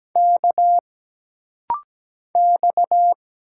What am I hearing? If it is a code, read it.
Morse code, heard as a keyed tone, about 17 wpm, 700 Hz. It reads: K X
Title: Single Letter-Number Alt - Rapid-Fire 17wpm